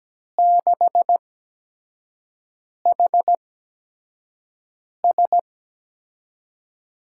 Morse code, heard as a keyed tone, about 17 wpm, spaced as for 5 wpm, 700 Hz. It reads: 6HS
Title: USA Call Signs - 8x Character Spacing - 17wpm